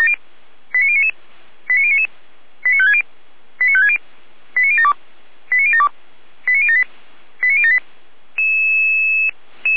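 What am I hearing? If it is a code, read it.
Five-tone selective call (ZVEI), heard as quick runs of five tones, tones of about 70 ms. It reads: 89099 89099 89560 89560 89072 89072 89078 89078
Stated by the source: ZVEI five-tone selective call received by SDR.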